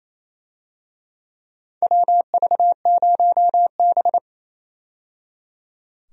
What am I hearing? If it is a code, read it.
Morse code, heard as a keyed tone, about 28 wpm, 700 Hz. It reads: WV0B
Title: USA Call Signs - 4 Characters - 28wpm